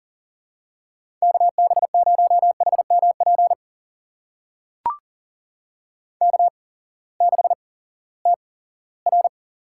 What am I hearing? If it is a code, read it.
Morse code, heard as a keyed tone, about 41 wpm, spaced as for 20 wpm, 700 Hz. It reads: KB0HMP K6TR
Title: USA Call Signs - 8x Character Spacing - 40wpm